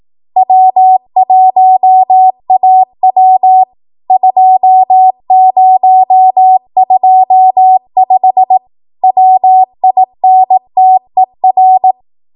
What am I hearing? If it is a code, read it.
Morse code, heard as a keyed tone, about 18 wpm, 750 Hz. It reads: W1AW 2025 WINTER
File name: Archive / 18 WPM / 18WPM